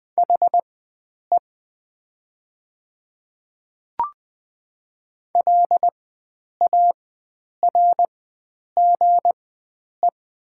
Morse code, heard as a keyed tone, about 20 wpm, spaced as for 9 wpm, 700 Hz. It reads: HE LARGE
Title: 4x Character Spacing - Top 300 Words - 20wpm